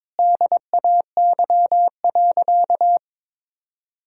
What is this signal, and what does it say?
Morse code, heard as a keyed tone, about 22 wpm, 700 Hz. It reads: DAY.
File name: practice / 22wpm / Sentences with 4-Word Phrases - 22wpm